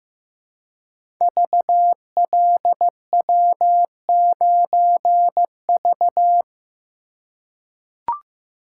Morse code, heard as a keyed tone, about 15 wpm, 700 Hz. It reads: VLW9V
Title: Code Groups - 5 Letter:Numbers - 15wpm